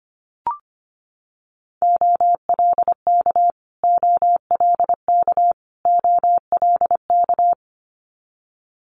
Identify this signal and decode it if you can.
Morse code, heard as a keyed tone, about 25 wpm, 700 Hz. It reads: OLK OLK OLK
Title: Multi-letter Phonemes Repeated 3x 25wpm